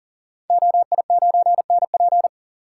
Morse code, heard as a keyed tone, about 40 wpm, 700 Hz. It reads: OI9NP
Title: Code Groups - 5 Letter:Numbers - 40wpm